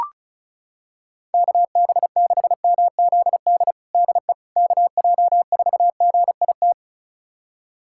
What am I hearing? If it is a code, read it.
Morse code, heard as a keyed tone, about 34 wpm, 700 Hz. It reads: KB6MZD DE KJ4GIT
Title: USA Callsign DE Callsign - 6 Characters - 35wpm